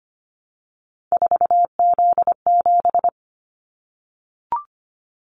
Morse code, heard as a keyed tone, about 25 wpm, 700 Hz. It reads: 4Z7
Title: Callsign Prefixes for Top 50 DXCC - 25wpm